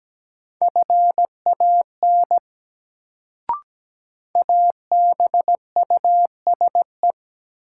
Morse code, heard as a keyed tone, about 17 wpm, 700 Hz. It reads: FAN ABUSE
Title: Sentences from Top 1300 Words - New Word Drill Alt - 17wpm